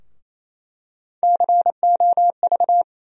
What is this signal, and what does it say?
Morse code, heard as a keyed tone, about 28 wpm, 700 Hz. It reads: COV
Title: USA Call Sign Postfixes - 3 Letters 28wpm